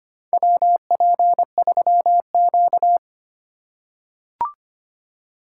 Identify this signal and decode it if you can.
Morse code, heard as a keyed tone, about 25 wpm, 700 Hz. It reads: WP3Q
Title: USA Call Signs - 4 Characters - 4x Character Spacing - 25wpm